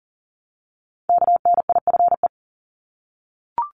Morse code, heard as a keyed tone, about 38 wpm, 700 Hz. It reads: KNIFE
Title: Multi-letter Phonemes Repeated 3x 40wpm